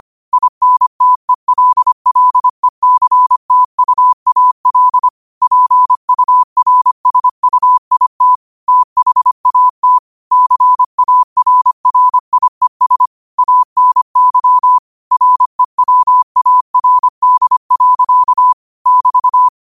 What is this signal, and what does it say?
Morse code, heard as a keyed tone, about 25 wpm, 1000 Hz. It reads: INTELLECTUAL PURSUIT THAT CARRIES ANY REWARD. <BT>
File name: qotdfiles / 25wpm